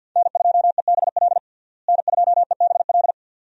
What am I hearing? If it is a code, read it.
Morse code, heard as a keyed tone, about 46 wpm, 700 Hz. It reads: N2EBL N2EBL
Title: USA Call Signs - 5 Characters - Repeated 2x - 50wpm